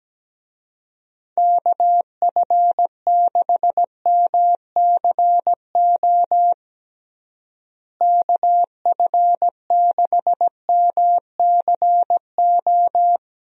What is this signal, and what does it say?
Morse code, heard as a keyed tone, about 17 wpm, 700 Hz. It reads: KF6MCO KF6MCO
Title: USA Call Signs - 6 Characters - Repeated 3x - 17wpm